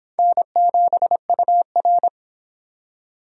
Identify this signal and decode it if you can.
Morse code, heard as a keyed tone, about 26 wpm, 700 Hz. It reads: N7UR